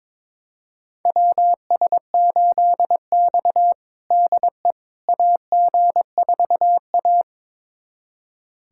Morse code, heard as a keyed tone, about 22 wpm, 700 Hz. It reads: WS8X DE AG4A
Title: USA Callsign DE Callsign - 4 Characters - 22wpm